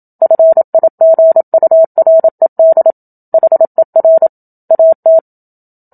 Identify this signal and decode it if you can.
Morse code, heard as a keyed tone, about 27 wpm, 650 Hz. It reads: FIGURED HER AT
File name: Book Two / 01-Chapter 1 - 27 WPM